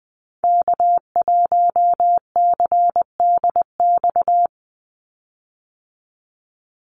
Morse code, heard as a keyed tone, about 20 wpm, 700 Hz. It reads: K1CDX
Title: Call Signs - 20wpm